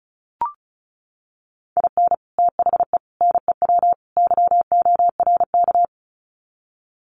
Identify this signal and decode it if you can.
Morse code, heard as a keyed tone, about 34 wpm, 700 Hz. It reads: IN THE NEW YORK
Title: Sets of 4 Words 35wpm